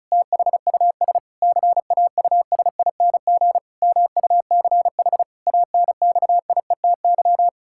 Morse code, heard as a keyed tone, about 36 wpm, 700 Hz. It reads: THUS CAUSING MUCH ANXIETY
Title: Pangram Challenge v2 - 35wpm